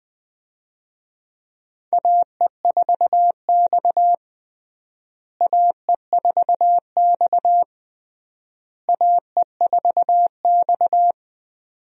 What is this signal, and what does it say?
Morse code, heard as a keyed tone, about 20 wpm, 700 Hz. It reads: AE4X AE4X AE4X
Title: USA Call Signs - 4 Characters - Repeated 3x - 20wpm